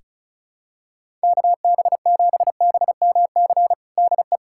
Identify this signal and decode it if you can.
Morse code, heard as a keyed tone, about 35 wpm, 700 Hz. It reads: KB7BMC DE
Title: USA Callsign DE Callsign - 6 Characters - 35wpm